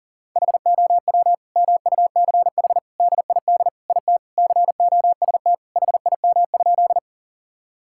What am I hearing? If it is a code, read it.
Morse code, heard as a keyed tone, about 40 wpm, 690 Hz. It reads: HOW MUCH DID IT COST HIM?
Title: Sentences from Top 1000 Words - Review Part 5 - 40wpm